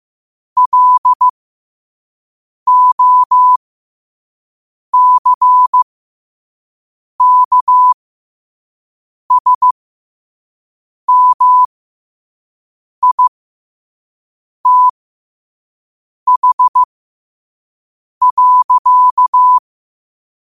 Morse code, heard as a keyed tone, about 15 wpm, spaced as for 5 wpm, 1000 Hz. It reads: LOCKSMITH.
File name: qotdfiles / 5wpm